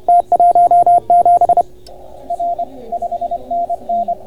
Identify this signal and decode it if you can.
Morse code, heard as a keyed tone, about 30 wpm, 690 Hz, with radio noise. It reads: T17 R 5NN